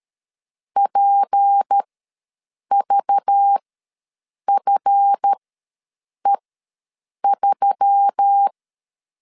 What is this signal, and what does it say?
Morse code, heard as a keyed tone, about 13 wpm, spaced as for 7 wpm, 800 Hz. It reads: PVFE3
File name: LEZIONI / 8 YJ45 / 8 YJ ws 12wpm 3342